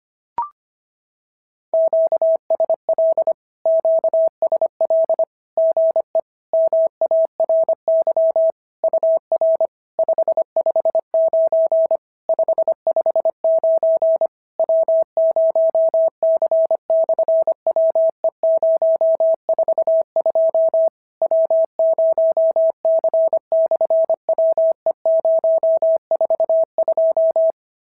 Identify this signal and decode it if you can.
Morse code, heard as a keyed tone, about 25 wpm, 650 Hz. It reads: QSL QSL GE MARY UR 559 559 W0C/WE042 W0C/WE042
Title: SOTA QSO Practice 25wpm